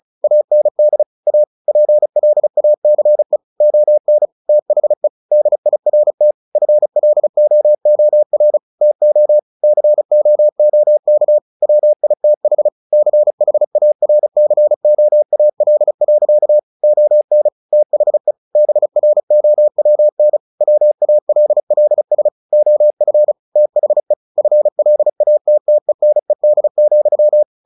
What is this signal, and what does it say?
Morse code, heard as a keyed tone, about 34 wpm, 600 Hz. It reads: AND A PLACE ON THE DIRT FLOOR TO COOK WITH CHARCOAL. ON THE BROWN WALLS OF THE FLATTENED,
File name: the old man and the sea / the old man and the sea 1 35wpm